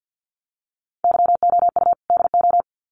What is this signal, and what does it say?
Morse code, heard as a keyed tone, about 50 wpm, 700 Hz. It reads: YOU DO
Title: Sets of 2 Words 50wpm